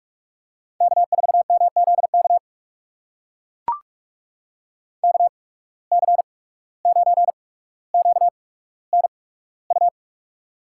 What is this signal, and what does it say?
Morse code, heard as a keyed tone, about 44 wpm, spaced as for 18 wpm, 700 Hz. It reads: K4MZK KC9QNU
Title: USA Call Signs - 8x Character Spacing - 45wpm